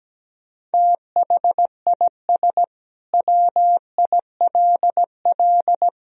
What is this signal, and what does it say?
Morse code, heard as a keyed tone, about 17 wpm, 700 Hz. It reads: THIS WILL
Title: Sentences from Top 400 Words and Abbreviations 17wpm